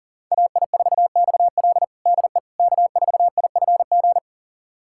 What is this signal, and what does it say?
Morse code, heard as a keyed tone, about 40 wpm, 700 Hz. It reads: AI4XL DE K4IFG
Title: USA Callsign DE Callsign - 5 Characters - 40wpm